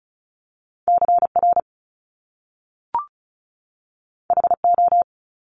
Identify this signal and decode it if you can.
Morse code, heard as a keyed tone, about 36 wpm, 700 Hz. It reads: CR HO
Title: Top Two Letter Combinations in Words 35wpm